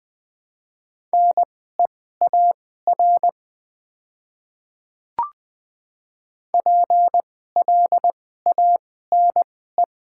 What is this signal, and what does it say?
Morse code, heard as a keyed tone, about 20 wpm, spaced as for 14 wpm, 700 Hz. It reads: NEAR PLANE
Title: 2x Character Spacing - Top 500 Words - 20wpm